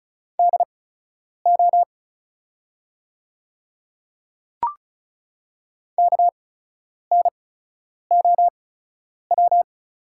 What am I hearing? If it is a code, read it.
Morse code, heard as a keyed tone, about 36 wpm, spaced as for 10 wpm, 700 Hz. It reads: DO KNOW
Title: 8x Character Spacing - Top 100 Words - 35wpm